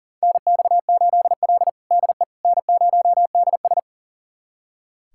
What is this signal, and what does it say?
Morse code, heard as a keyed tone, about 37 wpm, 700 Hz. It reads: NX8L DE N0DS
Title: USA Callsign DE Callsign - 4 Characters - 40wpm